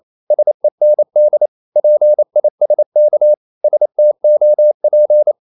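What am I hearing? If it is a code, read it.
Morse code, heard as a keyed tone, about 28 wpm, 600 Hz. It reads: SEND PISK STOP
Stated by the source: Danish short words sent in Morse at 28 wpm, from CW med Gnister.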